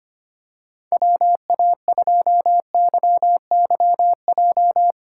Morse code, heard as a keyed tone, about 25 wpm, 700 Hz. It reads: WA2YYJ
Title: USA Call Signs - 6 Characters - 2x Character Spacing - 25wpm